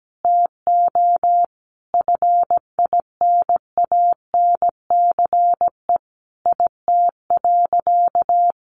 Morse code, heard as a keyed tone, about 17 wpm, 700 Hz. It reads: TO FINANCE IT.
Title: Sentences from Top 1500 Words 17wpm